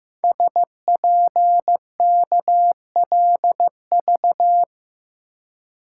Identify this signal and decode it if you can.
Morse code, heard as a keyed tone, about 15 wpm, 700 Hz. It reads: SPKLV
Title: Code Groups - 5 Letters - 15wpm